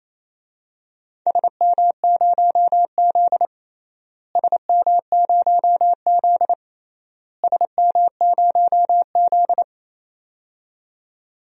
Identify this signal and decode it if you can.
Morse code, heard as a keyed tone, about 28 wpm, 700 Hz. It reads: SM0Z SM0Z SM0Z
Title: DXCC-100 Call Signs - 4 Characters - Repeated 3x - 28wpm